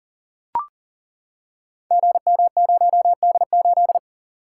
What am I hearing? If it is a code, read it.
Morse code, heard as a keyed tone, about 39 wpm, 700 Hz. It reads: GM0D8
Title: Code Groups - 5 Letter:Numbers - 40wpm